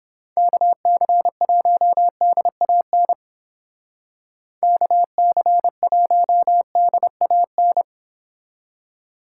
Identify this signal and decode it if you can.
Morse code, heard as a keyed tone, about 29 wpm, 700 Hz. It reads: KC1DAN KC1DAN